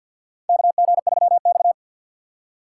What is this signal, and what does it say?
Morse code, heard as a keyed tone, about 47 wpm, 700 Hz. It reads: KG3X
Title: USA Call Signs - 4 Characters - 3x Character Spacing - 50wpm